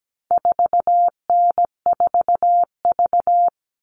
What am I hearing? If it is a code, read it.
Morse code, heard as a keyed tone, about 17 wpm, 700 Hz. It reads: 4N4V
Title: All Cut Numbers - 4-Digit 17wpm